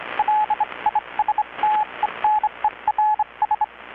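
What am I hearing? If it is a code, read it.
Morse code, heard as a keyed tone, about 25 wpm, 840 Hz, with radio noise. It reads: LISTENERS
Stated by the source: Maritime radiotelegraphy coast station event transmission, received in Chiapas, Mexico about 3,800 km away on a Kenwood R-600 receiver with a 10-metre grounded longwire antenna.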